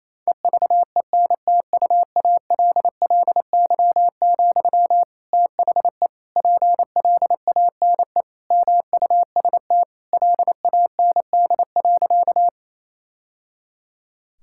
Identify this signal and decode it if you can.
Morse code, heard as a keyed tone, about 27 wpm, 700 Hz. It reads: EVENTUALLY, THE PLANE MUST LAND.
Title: Sentences from Top 1000 Words - Review Part 6 - 28wpm